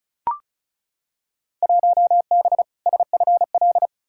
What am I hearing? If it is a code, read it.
Morse code, heard as a keyed tone, about 34 wpm, 700 Hz. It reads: 1B SFL
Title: Field Day Exchanges Encore - 35wpm